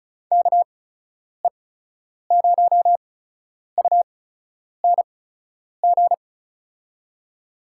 Morse code, heard as a keyed tone, about 36 wpm, spaced as for 10 wpm, 700 Hz. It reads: KE0UNG
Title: USA Call Signs - 6 Characters - 8x Character Spacing - 35wpm